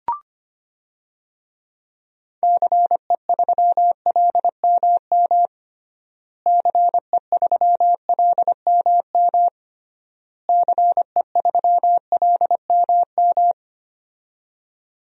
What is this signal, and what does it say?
Morse code, heard as a keyed tone, about 25 wpm, 700 Hz. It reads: CE3LMM CE3LMM CE3LMM
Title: DXCC-100 Call Signs - 6 Characters - Repeated 3x - 25wpm